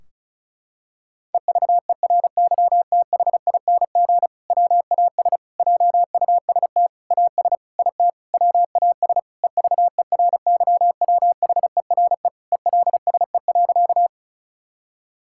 Morse code, heard as a keyed tone, about 36 wpm, 700 Hz. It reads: EVERYTHING WAS JUST AS IT WAS EVERYWHERE ELSE.
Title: Sentences from Top 1600 Words 35wpm